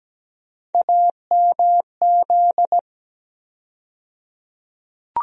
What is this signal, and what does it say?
Morse code, heard as a keyed tone, about 17 wpm, 700 Hz. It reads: AMZ
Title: USA Call Sign Postfixes - 3 Letters 17wpm